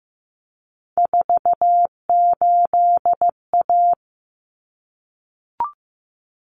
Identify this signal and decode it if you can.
Morse code, heard as a keyed tone, about 15 wpm, 700 Hz. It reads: 48A
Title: Common Cut Numbers - 3-Digit 15wpm